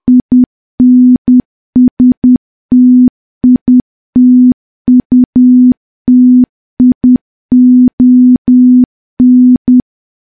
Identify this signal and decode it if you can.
Morse code, heard as a keyed tone, about 10 wpm, 250 Hz. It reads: INSTITUTION
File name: morse / morse-training-10WPM